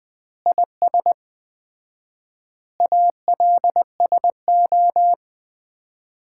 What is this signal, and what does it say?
Morse code, heard as a keyed tone, about 21 wpm, 700 Hz. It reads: IS ALSO
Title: Common QSO Phrases - 4x Word-Spacing - 20wpm